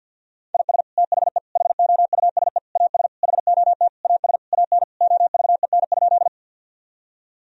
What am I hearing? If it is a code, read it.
Morse code, heard as a keyed tone, about 50 wpm, 700 Hz. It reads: IS THE HOUSE AS HOT AS AN OVEN?
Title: Sentences from Top 1000 Words - Review Part 4 - 50wpm